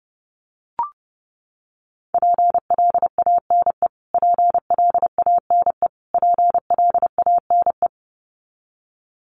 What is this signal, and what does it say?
Morse code, heard as a keyed tone, about 30 wpm, 700 Hz. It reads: PLANE PLANE PLANE
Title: Top 500 Words Repeated 30wpm